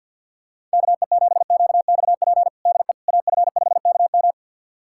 Morse code, heard as a keyed tone, about 49 wpm, 700 Hz. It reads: KE7QXP DE AF5KM
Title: USA Callsign DE Callsign 50wpm